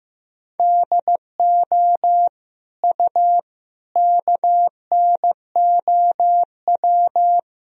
Morse code, heard as a keyed tone, about 15 wpm, 700 Hz. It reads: DO U KNOW
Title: Sentences from Top 100 Words and Abbreviations 15wpm